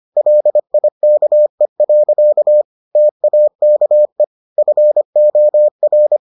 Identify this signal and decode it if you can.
Morse code, heard as a keyed tone, about 25 wpm, 600 Hz. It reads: LIKE. TAKE FOR